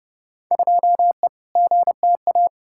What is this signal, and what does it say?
Morse code, heard as a keyed tone, about 31 wpm, 700 Hz. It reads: 2E GTA
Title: Field Day Exchanges Encore - 30wpm